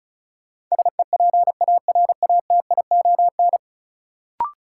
Morse code, heard as a keyed tone, about 34 wpm, 700 Hz. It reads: SEPARATION
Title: Sets of 5 Words - New Word Drill - 35wpm